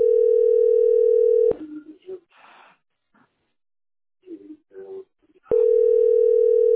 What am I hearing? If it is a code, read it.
Ringback tone (North America), 440 + 480 Hz.